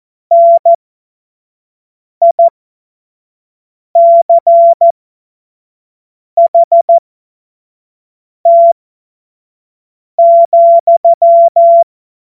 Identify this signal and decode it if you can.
Morse code, heard as a keyed tone, about 14 wpm, spaced as for 5 wpm, 680 Hz. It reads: NICHT,
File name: Morsekurs / Witze / Witz 25. Januar 2022, 25 F70